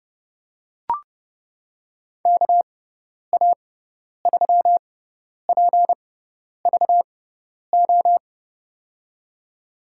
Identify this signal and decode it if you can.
Morse code, heard as a keyed tone, about 30 wpm, spaced as for 10 wpm, 700 Hz. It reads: KA3PVO
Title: USA Call Signs - 6 Characters - 6x Character Spacing - 30wpm